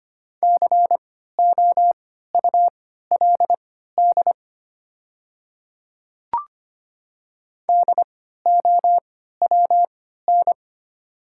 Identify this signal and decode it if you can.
Morse code, heard as a keyed tone, about 25 wpm, spaced as for 14 wpm, 700 Hz. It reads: COULD DOWN
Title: 3x Character Spacing - Top 100 Words - 25wpm